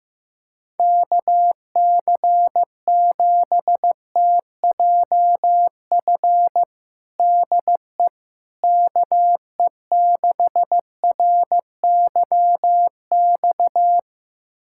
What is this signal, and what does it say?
Morse code, heard as a keyed tone, about 16 wpm, 700 Hz. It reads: KC7TJF DE KE6RYX
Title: USA Callsign DE Callsign - 6 Characters - 15wpm